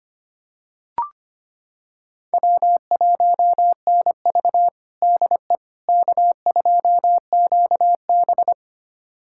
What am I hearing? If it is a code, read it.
Morse code, heard as a keyed tone, about 25 wpm, 700 Hz. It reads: W1NV DE K2QB
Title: USA Callsign DE Callsign - 4 Characters - 25wpm